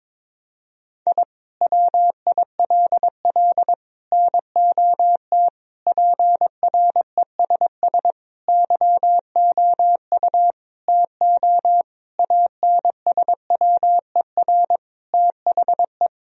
Morse code, heard as a keyed tone, about 22 wpm, 700 Hz. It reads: I WILL NOT PRESS YOU TO ANSWER THE
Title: Sentences from Top 1000 Words - Review Part 1 - 22wpm